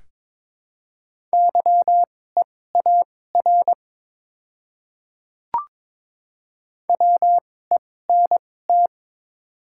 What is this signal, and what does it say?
Morse code, heard as a keyed tone, about 22 wpm, spaced as for 16 wpm, 700 Hz. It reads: YEAR WENT
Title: 2x Character Spacing - Top 300 Words - 22wpm